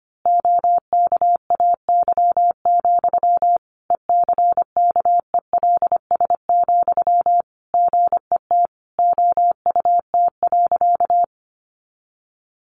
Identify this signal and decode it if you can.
Morse code, heard as a keyed tone, about 25 wpm, 700 Hz. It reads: OKAY, ECKELS, GET OUT.
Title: A Sound Of Thunder 25wpm